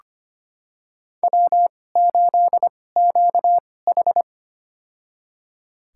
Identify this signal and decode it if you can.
Morse code, heard as a keyed tone, about 25 wpm, spaced as for 18 wpm, 700 Hz. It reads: W8QH
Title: USA Call Signs - 4 Characters - 2x Character Spacing - 25wpm